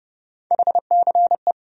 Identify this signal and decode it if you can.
Morse code, heard as a keyed tone, about 30 wpm, 700 Hz. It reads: HCE
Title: USA Call Sign Postfixes - 3 Letters 30wpm